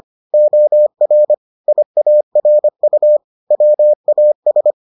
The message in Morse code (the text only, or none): OR IARU WAS